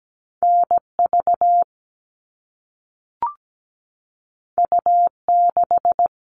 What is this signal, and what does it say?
Morse code, heard as a keyed tone, about 17 wpm, 700 Hz. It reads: NV U6
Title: All Cut Numbers - 2-Digit 17wpm